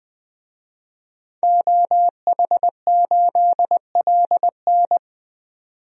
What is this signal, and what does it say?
Morse code, heard as a keyed tone, about 21 wpm, 700 Hz. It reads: OH8LN
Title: DXCC-100 Call Signs - 5 Characters - 20wpm